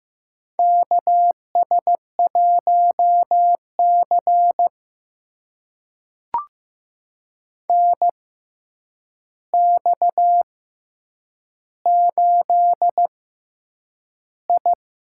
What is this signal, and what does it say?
Morse code, heard as a keyed tone, about 15 wpm, spaced as for 8 wpm, 700 Hz. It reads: KS1C NX8I
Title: USA Call Signs - 4 Characters - 6x Character Spacing - 15wpm